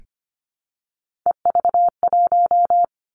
Morse code, heard as a keyed tone, about 25 wpm, 700 Hz. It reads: EV1
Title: Three Challenging Letter-Number Combinations 25wpm